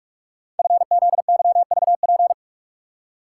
Morse code, heard as a keyed tone, about 44 wpm, 690 Hz. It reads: FZYVP
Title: Code Groups - 5 Letter:Numbers - 45wpm